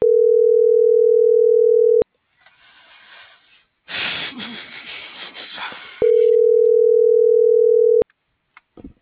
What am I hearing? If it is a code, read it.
Ringback tone (North America), 440 + 480 Hz.